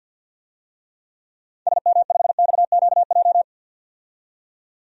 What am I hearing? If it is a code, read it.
Morse code, heard as a keyed tone, about 46 wpm, 700 Hz. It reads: SM5XQJ
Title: DXCC-100 Call Signs - 6 Characters - 50wpm